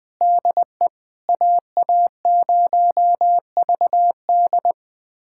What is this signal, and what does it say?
Morse code, heard as a keyed tone, about 21 wpm, 700 Hz. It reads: DE AA0VD
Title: USA Callsign DE Callsign - 5 Characters - 20wpm